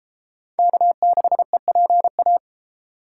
Morse code, heard as a keyed tone, about 30 wpm, 700 Hz. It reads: K6EPA